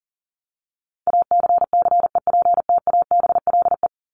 Morse code, heard as a keyed tone, about 37 wpm, 700 Hz. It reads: ACCEPTABLE
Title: Top 1501-1750 Words 40wpm